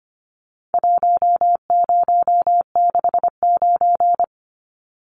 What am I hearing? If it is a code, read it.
Morse code, heard as a keyed tone, about 25 wpm, 700 Hz. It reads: 1069
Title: Numbers - Contest Style - 4-Digit 25wpm